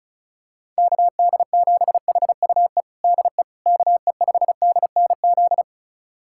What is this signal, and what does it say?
Morse code, heard as a keyed tone, about 34 wpm, 700 Hz. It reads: KD7HUE DE KE5DNZ
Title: USA Callsign DE Callsign - 6 Characters - 35wpm